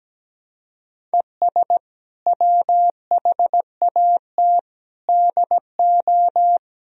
Morse code, heard as a keyed tone, about 17 wpm, 700 Hz. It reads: ES WHAT DO
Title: Sentences from Top 100 Words and Abbreviations 17wpm